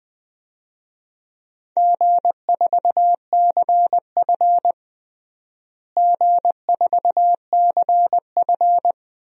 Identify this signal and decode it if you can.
Morse code, heard as a keyed tone, about 20 wpm, 700 Hz. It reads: G4CF G4CF
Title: DXCC-100 Call Signs - 4 Characters - Repeated 3x - 20wpm